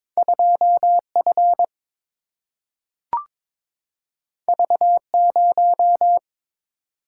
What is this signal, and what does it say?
Morse code, heard as a keyed tone, about 22 wpm, 700 Hz. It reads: 2 F V 0
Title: Single-Letter Single-Number 22wpm